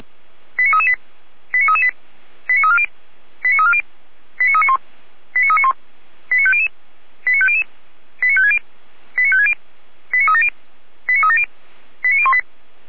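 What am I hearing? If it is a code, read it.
Five-tone selective call (ZVEI), heard as quick runs of five tones, tones of about 70 ms. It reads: 89338 89338 89340 89340 89391 89391 89500 89500 89560 89560 89370 89370 89018
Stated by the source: ZVEI five-tone call received on SDR.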